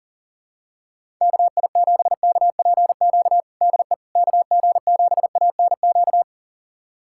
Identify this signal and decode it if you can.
Morse code, heard as a keyed tone, about 37 wpm, 700 Hz. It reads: KI7KPQ DE KG7ANQ
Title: USA Callsign DE Callsign - 6 Characters - 40wpm